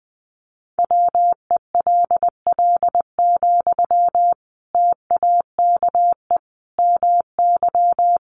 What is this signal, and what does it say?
Morse code, heard as a keyed tone, about 20 wpm, 700 Hz. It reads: WELL, TAKE MY
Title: Sentences from Top 1100 Words 20wpm